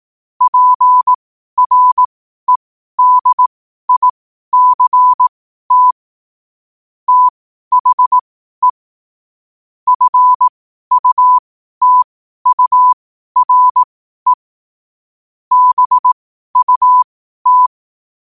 Morse code, heard as a keyed tone, about 18 wpm, spaced as for 12 wpm, 1000 Hz. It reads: PREDICT THE FUTURE BUT
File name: qotdfiles / 12wpm